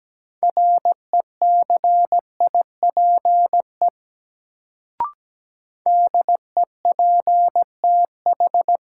Morse code, heard as a keyed tone, about 17 wpm, 700 Hz. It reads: RECIPE DEPTH
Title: Sentences from Top 1200 Words - New Word Drill - 17wpm